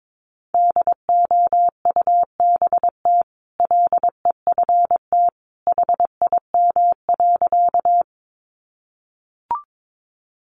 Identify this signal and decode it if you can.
Morse code, heard as a keyed tone, about 22 wpm, 700 Hz. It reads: DOUBT LEFT HIM.
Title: Sentences from Top 1500 Words 22wpm